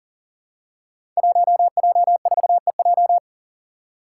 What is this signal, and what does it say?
Morse code, heard as a keyed tone, about 41 wpm, 700 Hz. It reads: 1J4EJ
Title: Five Challenging Letter-Number Combinations 40wpm